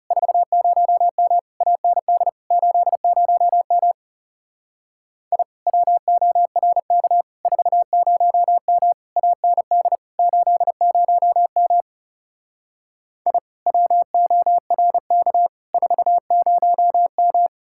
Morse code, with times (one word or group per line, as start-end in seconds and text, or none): 0.1-1.4 40M
1.6-2.3 AND
2.5-3.9 80M
5.3-5.4 I
5.7-7.2 WORK
7.5-8.9 40M
9.2-10.0 AND
10.2-11.8 80M
13.3-13.4 I
13.7-15.5 WORK
15.7-17.5 40M